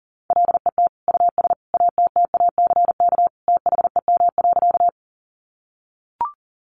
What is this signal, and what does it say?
Morse code, heard as a keyed tone, about 40 wpm, 700 Hz. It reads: LET US ATTACK THEM.
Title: Sentences from Top 1500 Words 40wpm